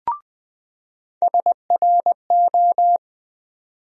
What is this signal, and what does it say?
Morse code, heard as a keyed tone, about 20 wpm, 700 Hz. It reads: SRO
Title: USA Call Sign Postfixes - 3 Letters 20wpm